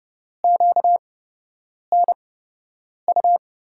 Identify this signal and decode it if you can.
Morse code, heard as a keyed tone, about 30 wpm, spaced as for 8 wpm, 700 Hz. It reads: QNU
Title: USA Call Signs - 8x Character Spacing - 30wpm